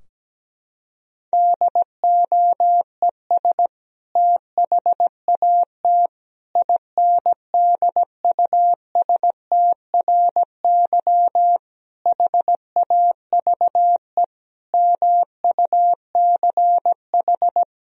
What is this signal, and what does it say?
Morse code, heard as a keyed tone, about 17 wpm, 700 Hz. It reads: DOES THAT INDUSTRY HAVE MUCH
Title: Sentences from Top 1000 Words - Review Part 4 - 17wpm